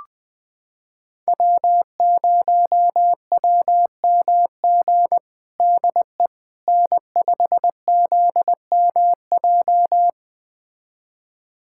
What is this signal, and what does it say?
Morse code, heard as a keyed tone, about 20 wpm, 700 Hz. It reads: W0WMG DE N5ZMJ
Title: USA Callsign DE Callsign - 5 Characters - 20wpm